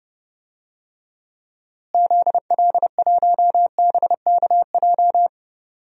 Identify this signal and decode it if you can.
Morse code, heard as a keyed tone, about 32 wpm, 700 Hz. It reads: ZL1BKJ
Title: DXCC-100 Call Signs - 6 Characters - Repeated 2x - 30wpm